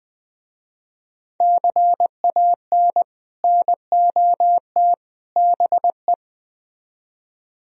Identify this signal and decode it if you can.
Morse code, heard as a keyed tone, about 20 wpm, 700 Hz. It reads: CAN NOT BE